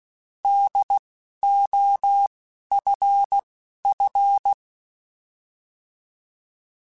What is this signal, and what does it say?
Morse code, heard as a keyed tone, about 16 wpm, spaced as for 12 wpm, 800 Hz. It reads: DOFF
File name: Alphabetical / Alphabetical mp3s / Ds @10